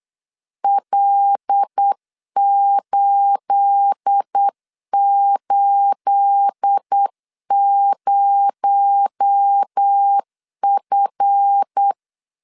Morse code, heard as a keyed tone, about 8 wpm, 800 Hz. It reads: L880F
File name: CW / DET / 1 9 1500C S C 8wpm 5653